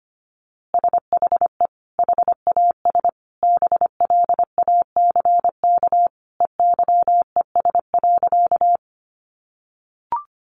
Morse code, heard as a keyed tone, about 25 wpm, 700 Hz. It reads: SHE HAS BLACK EYES.
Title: Easy Phrases 25wpm